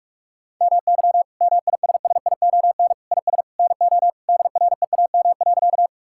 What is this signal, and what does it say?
Morse code, heard as a keyed tone, about 46 wpm, 700 Hz. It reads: MY MISSION IS NO DREAM.
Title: Sentences from Top 1600 Words 45wpm